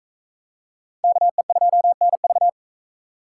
Morse code, heard as a keyed tone, about 39 wpm, 700 Hz. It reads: KE2NV